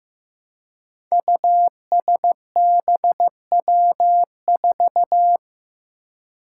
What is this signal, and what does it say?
Morse code, heard as a keyed tone, about 15 wpm, 700 Hz. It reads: USBW4
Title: Five Challenging Letter-Number Combinations 15wpm